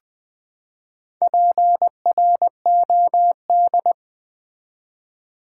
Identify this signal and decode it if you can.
Morse code, heard as a keyed tone, about 20 wpm, 700 Hz. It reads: PROD
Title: Build A Word - 4 Letters At A Time - 12-Letter Words - 20wpm